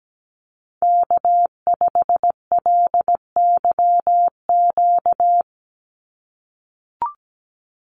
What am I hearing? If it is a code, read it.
Morse code, heard as a keyed tone, about 17 wpm, 700 Hz. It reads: K5LYQ
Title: USA Call Signs 17wpm